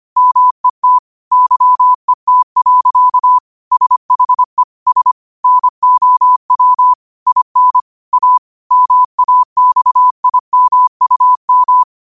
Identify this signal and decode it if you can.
Morse code, heard as a keyed tone, about 25 wpm, 1000 Hz. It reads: MET YET. SHE S NOW IN A MAXIMUM
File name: qotdfiles / 25wpm